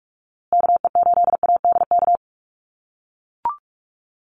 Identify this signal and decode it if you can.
Morse code, heard as a keyed tone, about 44 wpm, 700 Hz. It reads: KE8ADK
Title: USA Call Signs Repeated 2x 45wpm